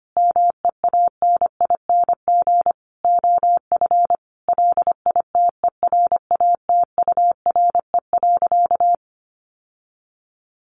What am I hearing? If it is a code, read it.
Morse code, heard as a keyed tone, about 25 wpm, 700 Hz. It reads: MEANING OF LITERATURE.
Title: Sentences from Top 900 Words 25wpm